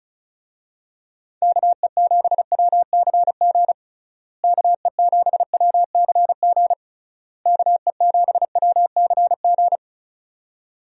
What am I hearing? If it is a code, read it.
Morse code, heard as a keyed tone, about 36 wpm, 700 Hz. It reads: KE7WCG KE7WCG KE7WCG
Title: USA Call Signs - 6 Characters - Repeated 3x - 35wpm